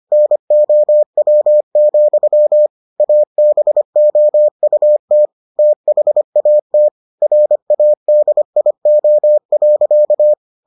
Morse code, heard as a keyed tone, about 25 wpm, 600 Hz. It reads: NOW, ABOUT THAT RADIO.